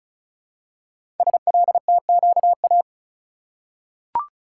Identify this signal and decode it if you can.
Morse code, heard as a keyed tone, about 36 wpm, 700 Hz. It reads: SLTQA
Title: Code Groups - 5 Letters - 35wpm